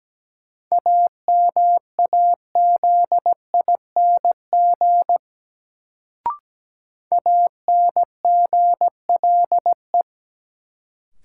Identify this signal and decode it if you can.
Morse code, heard as a keyed tone, about 17 wpm, 700 Hz. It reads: AMAZING ANGLE
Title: Sentences from Top 1500 Words - New Word Drill Alt - 17wpm